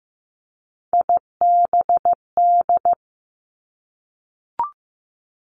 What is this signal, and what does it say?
Morse code, heard as a keyed tone, about 15 wpm, 700 Hz. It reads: IBD
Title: Three Challenging Letter-Number Combinations 15wpm